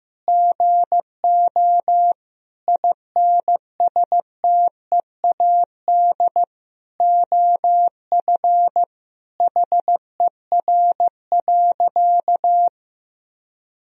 Morse code, heard as a keyed tone, about 15 wpm, 700 Hz. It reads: GO INSTEAD OF HER.
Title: Sentences from Top 1000 Words - Review Part 4 - 15wpm